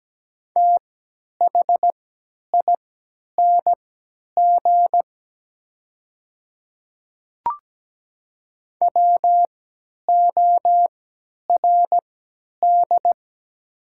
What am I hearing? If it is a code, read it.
Morse code, heard as a keyed tone, about 17 wpm, spaced as for 10 wpm, 700 Hz. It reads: THING WORD
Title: 3x Character Spacing - Top 100 Words - 17wpm